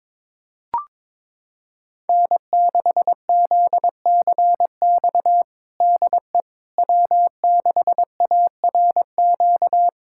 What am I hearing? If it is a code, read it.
Morse code, heard as a keyed tone, about 22 wpm, 700 Hz. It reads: N6ZCX DE W6ARQ
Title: USA Callsign DE Callsign - 5 Characters - 22wpm